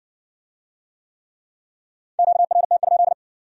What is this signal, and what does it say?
Morse code, heard as a keyed tone, about 66 wpm, 700 Hz. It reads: QRT?